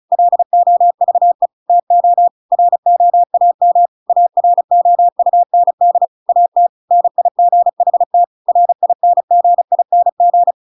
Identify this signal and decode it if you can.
Morse code, heard as a keyed tone, about 34 wpm, 700 Hz. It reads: LOVE TO ROAM AROUND AT NIGHT RINGING